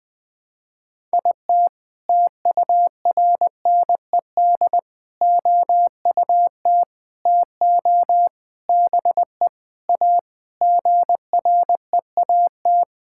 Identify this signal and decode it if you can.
Morse code, heard as a keyed tone, about 20 wpm, 700 Hz. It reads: IT TURNED OUT TO BE A GREAT
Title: Sentences with 6-Word Phrases - 20wpm